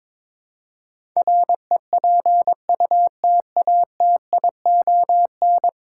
Morse code, heard as a keyed tone, about 21 wpm, 700 Hz. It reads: REPUTATION
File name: practice / 22wpm / 10 Letter Words - First 3 Letters Given - 22wpm